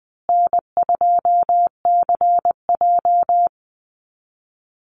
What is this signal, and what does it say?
Morse code, heard as a keyed tone, about 20 wpm, 700 Hz. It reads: N2CJ
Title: Easy Call Signs Repeated 3x - 20wpm